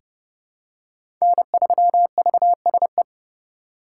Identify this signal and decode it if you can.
Morse code, heard as a keyed tone, about 32 wpm, 700 Hz. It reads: N3VSE
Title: USA Call Signs - 5 Characters - 3x Character Spacing - 30wpm